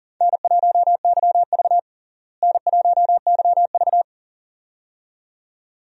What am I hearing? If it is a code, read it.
Morse code, heard as a keyed tone, about 40 wpm, 700 Hz. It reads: N1YV N1YV
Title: USA Call Signs - 4 Characters - Repeated 2x - 40wpm